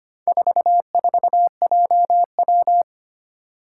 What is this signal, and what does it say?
Morse code, heard as a keyed tone, about 25 wpm, 700 Hz. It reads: 44JW
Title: Four Challenging Letter-Number Combinations 25wpm